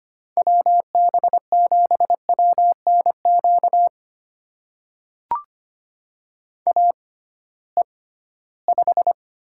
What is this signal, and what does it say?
Morse code, heard as a keyed tone, about 25 wpm, spaced as for 16 wpm, 700 Hz. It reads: WB7WNQ AE5
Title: USA Call Signs - 6x Character Spacing - 25wpm